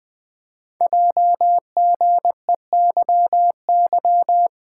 Morse code, heard as a keyed tone, about 20 wpm, 700 Hz. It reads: JGEYY
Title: Code Groups - 5 Letters - 20wpm